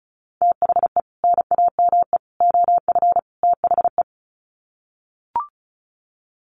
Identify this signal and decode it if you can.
Morse code, heard as a keyed tone, about 34 wpm, 700 Hz. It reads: THE NAME OF THE
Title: Sets of 4 Words 35wpm